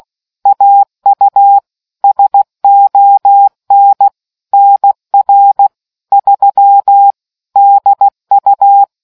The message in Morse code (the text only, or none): AU SON NR 3 DU